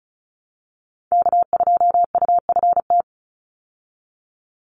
Morse code, heard as a keyed tone, about 34 wpm, 700 Hz. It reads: K2UFT
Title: Call Signs Repeated 3x - 35wpm